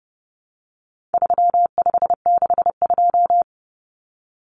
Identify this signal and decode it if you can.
Morse code, heard as a keyed tone, about 30 wpm, 700 Hz. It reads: 3562
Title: Numbers - 4-Digit 30wpm